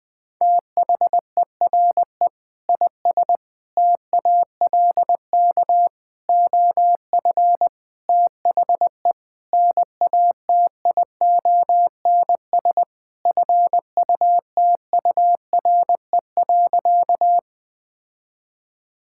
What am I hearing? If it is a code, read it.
Morse code, heard as a keyed tone, about 20 wpm, 700 Hz. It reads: THERE IS TALK OF THE NATIONS FUTURE.
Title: Sentences with 5-Word Phrases - Spoken Beginning - 20wpm